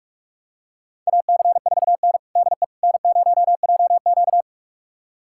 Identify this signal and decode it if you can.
Morse code, heard as a keyed tone, about 45 wpm, 700 Hz. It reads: AC4N DE N0JQ
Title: USA Callsign DE Callsign - 4 Characters - 45wpm